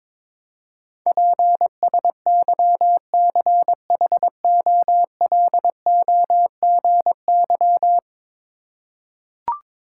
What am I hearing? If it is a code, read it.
Morse code, heard as a keyed tone, about 22 wpm, 700 Hz. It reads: PSYCHOLOGY
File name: practice / 22wpm / 10 Letter Words - First 3 Letters Given - 22wpm